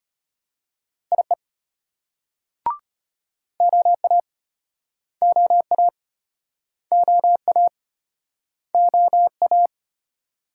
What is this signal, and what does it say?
Morse code, heard as a keyed tone, about 33 wpm, 700 Hz. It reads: IE OA OA OA OA